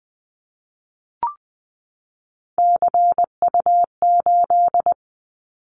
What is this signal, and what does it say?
Morse code, heard as a keyed tone, about 20 wpm, 700 Hz. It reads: CU8
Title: Callsign Prefixes for Top 100 DXCC - 20wpm